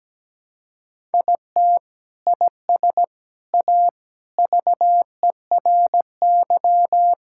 Morse code, heard as a keyed tone, about 17 wpm, 700 Hz. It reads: IT IS A VERY
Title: Sentences from Top 1000 Words - Review Part 5 - 17wpm